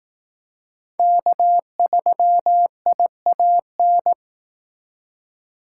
Morse code, heard as a keyed tone, about 19 wpm, 700 Hz. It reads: K3IAN